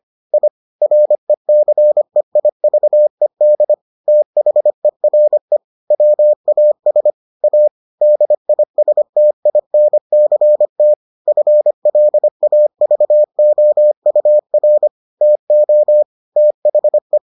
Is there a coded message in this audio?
Yes: Morse code from 0.3 to 17.2 s, heard as a keyed tone, about 25 wpm, 600 Hz. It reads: I RECEIVED THERE WAS A DISTINCT FLAVOUR TO THE